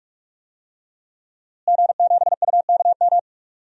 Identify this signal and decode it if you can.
Morse code, heard as a keyed tone, about 44 wpm, 700 Hz. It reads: G7UKM
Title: DXCC-100 Call Signs - 5 Characters - Repeated 2x - 45wpm